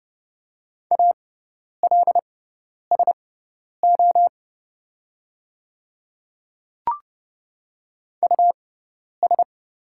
Morse code, heard as a keyed tone, about 30 wpm, spaced as for 10 wpm, 700 Hz. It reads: ALSO US
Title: 6x Character Spacing - Top 300 Words - 30wpm